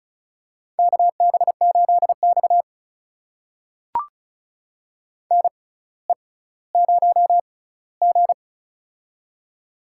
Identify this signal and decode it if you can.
Morse code, heard as a keyed tone, about 34 wpm, spaced as for 18 wpm, 700 Hz. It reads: KB8X NE0G
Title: USA Call Signs - 4 Characters - 6x Character Spacing - 35wpm